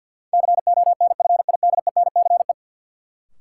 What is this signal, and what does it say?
Morse code, heard as a keyed tone, about 48 wpm, 700 Hz. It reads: CONFIDENCE
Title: Sentences from Top 1500 Words - New Word Drill Alt - 50wpm